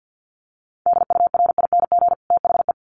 Morse code, heard as a keyed tone, about 50 wpm, 700 Hz. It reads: DURING THE
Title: Sets of 2 Words 50wpm